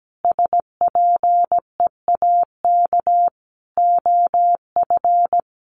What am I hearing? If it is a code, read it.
Morse code, heard as a keyed tone, about 17 wpm, 700 Hz. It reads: SPEAK OF
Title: Sentences from Top 1500 Words 17wpm